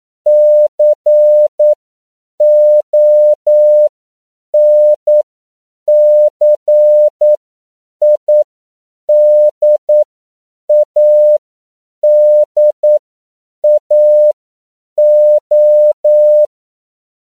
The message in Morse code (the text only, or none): CONCIDADAO